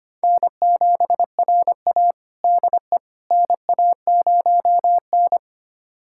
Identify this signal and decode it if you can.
Morse code, heard as a keyed tone, about 25 wpm, 700 Hz. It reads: N7RA DE NA0N
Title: USA Callsign DE Callsign - 4 Characters - 25wpm